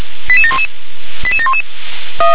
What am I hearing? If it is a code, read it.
Five-tone selective call (ZVEI), heard as quick runs of five tones, tones of about 70 ms. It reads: 88711 88711